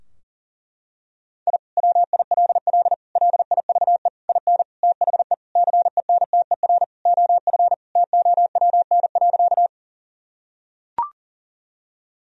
Morse code, heard as a keyed tone, about 40 wpm, 700 Hz. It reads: I WILL LIVE IN THE CENTER OF TOWN.
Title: Sentences from Top 1000 Words - Review Part 2 - 40wpm